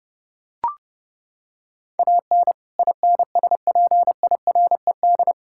Morse code, heard as a keyed tone, about 30 wpm, 700 Hz. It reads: AN INSPIRED
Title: Pangram Challenge v2 - 30wpm